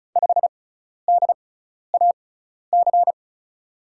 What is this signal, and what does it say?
Morse code, heard as a keyed tone, about 36 wpm, spaced as for 12 wpm, 700 Hz. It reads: 5DAC
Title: USA Call Signs - 5 Characters - 6x Character Spacing - 35wpm